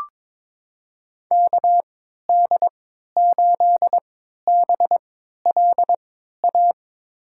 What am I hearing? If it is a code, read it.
Morse code, heard as a keyed tone, about 22 wpm, spaced as for 13 wpm, 700 Hz. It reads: KD8BLA
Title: USA Call Signs - 3x Character Spacing - 22wpm